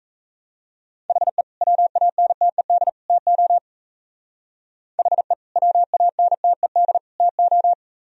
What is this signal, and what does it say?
Morse code, heard as a keyed tone, about 35 wpm, 700 Hz. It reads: HE WANTED TO HE WANTED TO